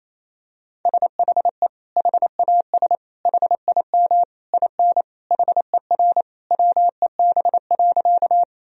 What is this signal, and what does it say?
Morse code, heard as a keyed tone, about 28 wpm, 700 Hz. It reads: SHE HAS HIM IN HER WEB.
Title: Sentences from Top 1000 Words - Review Part 4 - 28wpm